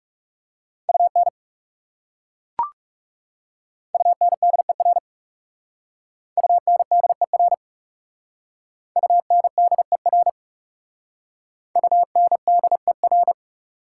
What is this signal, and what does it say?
Morse code, heard as a keyed tone, about 37 wpm, 700 Hz. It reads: UN UNDER UNDER UNDER UNDER
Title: Speed-Racing Prefixes and Suffixes 30wpm